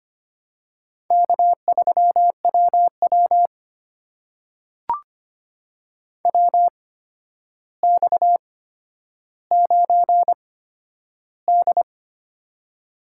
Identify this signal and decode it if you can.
Morse code, heard as a keyed tone, about 25 wpm, spaced as for 11 wpm, 700 Hz. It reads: K3WW WX9D
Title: USA Call Signs - 4 Characters - 8x Character Spacing - 25wpm